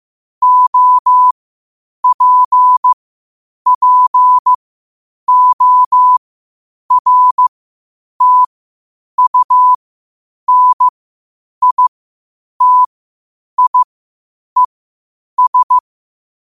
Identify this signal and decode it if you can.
Morse code, heard as a keyed tone, about 15 wpm, spaced as for 8 wpm, 1000 Hz. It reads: OPPORTUNITIES
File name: qotdfiles / 8wpm